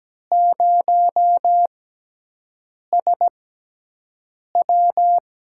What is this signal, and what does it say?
Morse code, heard as a keyed tone, about 17 wpm, spaced as for 6 wpm, 700 Hz. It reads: 0SW
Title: USA Call Signs - 5 Characters - 6x Character Spacing - 17wpm